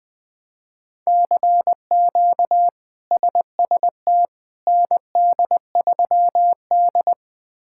Morse code, heard as a keyed tone, about 21 wpm, 700 Hz. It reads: CQ SST ND3D
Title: K1USN SST QSO Practice - 20wpm